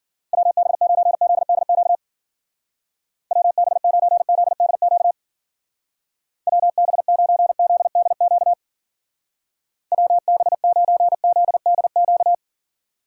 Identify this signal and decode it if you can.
Morse code, heard as a keyed tone, about 50 wpm, 700 Hz. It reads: WB9ZDQ WB9ZDQ WB9ZDQ WB9ZDQ
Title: Speed Racing - USA Call Signs 40wpm